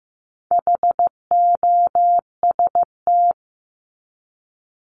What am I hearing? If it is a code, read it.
Morse code, heard as a keyed tone, about 15 wpm, 700 Hz. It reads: HOST
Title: Top 1501-1750 Words 15wpm